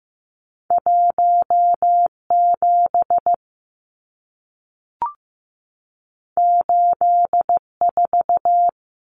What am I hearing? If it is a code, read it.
Morse code, heard as a keyed tone, about 15 wpm, 700 Hz. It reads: 1 7 8 4
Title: Numbers - 2-Digit 15wpm